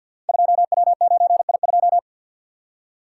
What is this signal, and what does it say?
Morse code, heard as a keyed tone, about 50 wpm, 700 Hz. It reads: 2W9I2
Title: Code Groups - 5 Letter:Numbers - 50wpm